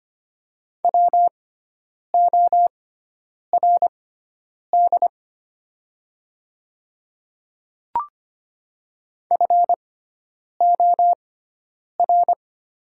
Morse code, heard as a keyed tone, about 25 wpm, spaced as for 9 wpm, 700 Hz. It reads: WORD FOR
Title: 6x Character Spacing - Top 200 Words - 25wpm